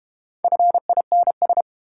Morse code, heard as a keyed tone, about 30 wpm, 700 Hz. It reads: FINS
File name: Easy Words - Speed-Racing with Hints - 25wpm